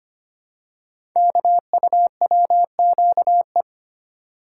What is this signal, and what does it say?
Morse code, heard as a keyed tone, about 25 wpm, 700 Hz. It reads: KUWQE
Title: Code Groups - 5 Letters - 25wpm